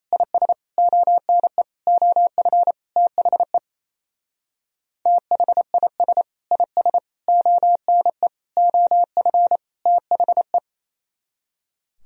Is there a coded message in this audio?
Yes: Morse code from 0.1 to 10.6 s, heard as a keyed tone, about 36 wpm, 700 Hz. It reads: IS ONE OF THE THIS IS ONE OF THE